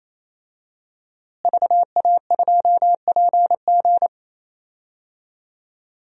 Morse code, heard as a keyed tone, about 28 wpm, 700 Hz. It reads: VA2PG
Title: DXCC-100 Call Signs - 5 Characters - Repeated 3x - 28wpm